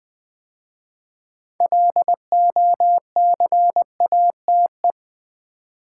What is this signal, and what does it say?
Morse code, heard as a keyed tone, about 20 wpm, 700 Hz. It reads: LOCATE
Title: 6 Letter Words - First 3 Letters Given - 20wpm